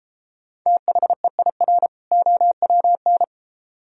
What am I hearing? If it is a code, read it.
Morse code, heard as a keyed tone, about 30 wpm, 700 Hz. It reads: THEIR OWN